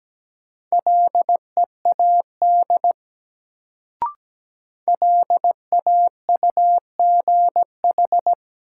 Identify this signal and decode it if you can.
Morse code, heard as a keyed tone, about 17 wpm, 700 Hz. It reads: LEAD LAUGH
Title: Sentences from Top 500 Words - New Word Drill - 17wpm